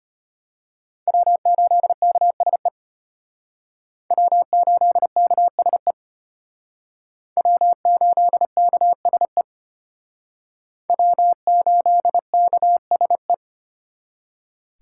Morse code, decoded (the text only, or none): W8KSE W8KSE W8KSE W8KSE